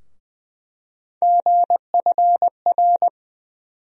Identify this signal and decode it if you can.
Morse code, heard as a keyed tone, about 20 wpm, 700 Hz. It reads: GFR
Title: USA Call Sign Postfixes - 3 Letters 20wpm